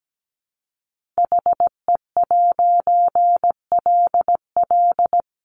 Morse code, heard as a keyed tone, about 17 wpm, 700 Hz. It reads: HE'LL
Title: A Sound Of Thunder 17wpm